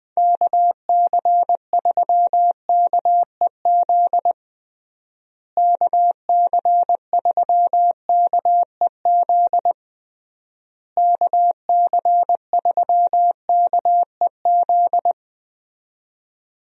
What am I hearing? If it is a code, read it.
Morse code, heard as a keyed tone, about 20 wpm, 700 Hz. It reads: KC3KEZ KC3KEZ KC3KEZ
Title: USA Call Signs - 6 Characters - Repeated 3x - 20wpm